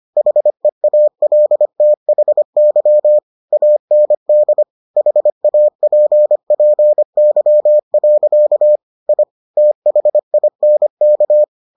Morse code, heard as a keyed tone, about 25 wpm, 600 Hz. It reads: HEALTHY AND HAPPY. I THINK